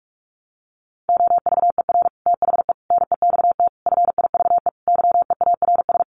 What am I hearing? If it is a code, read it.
Morse code, heard as a keyed tone, about 44 wpm, 700 Hz. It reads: OVER THE NEXT FIVE YEARS
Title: Sets of 5 Words 45wpm